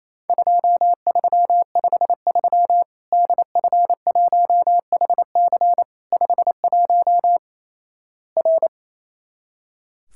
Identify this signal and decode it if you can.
Morse code, heard as a keyed tone, about 27 wpm, 700 Hz. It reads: 2353 DF1HC 51 R
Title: Worked All Europe DX Contest - QTC Exchange - 28wpm